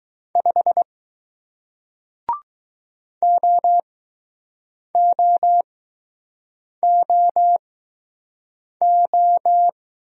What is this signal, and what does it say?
Morse code, heard as a keyed tone, about 19 wpm, spaced as for 6 wpm, 700 Hz. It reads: 5 OOOO